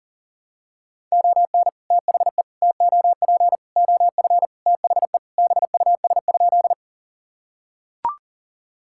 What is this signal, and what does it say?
Morse code, heard as a keyed tone, about 37 wpm, 700 Hz. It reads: ON THE TOP OF THE BUS?
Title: Sentences from Top 1000 Words - Review Part 6 - 40wpm